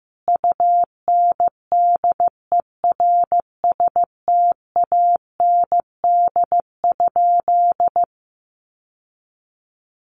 Morse code, heard as a keyed tone, about 15 wpm, 700 Hz. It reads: UNDERSTAND?
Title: Sentences from Top 1500 Words 15wpm